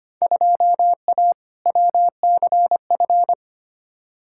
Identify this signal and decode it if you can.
Morse code, heard as a keyed tone, about 25 wpm, 700 Hz. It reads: 2A WCF
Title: Field Day Exchanges - 25wpm